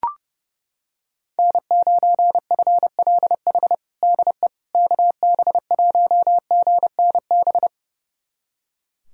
Morse code, heard as a keyed tone, about 30 wpm, 700 Hz. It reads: N9FLH DE KB1GNB
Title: USA Callsign DE Callsign 30wpm